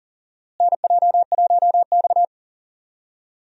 Morse code, heard as a keyed tone, about 40 wpm, 700 Hz. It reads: NJ1X
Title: USA Call Signs - 4 Characters - Repeated 3x - 40wpm